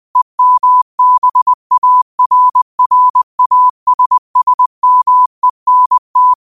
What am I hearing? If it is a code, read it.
Morse code, heard as a keyed tone, about 20 wpm, 1000 Hz. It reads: EMBARRASSMENT